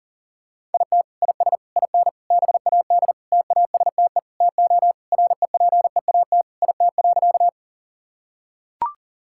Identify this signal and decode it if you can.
Morse code, heard as a keyed tone, about 40 wpm, 700 Hz. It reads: IT IS IN BAD TASTE TO REPEAT IT.
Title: Sentences from Top 1600 Words 40wpm